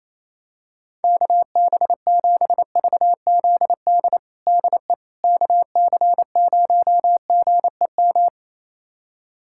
Morse code, heard as a keyed tone, about 28 wpm, 700 Hz. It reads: KB7VZD DE KC0GEM
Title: USA Callsign DE Callsign 28wpm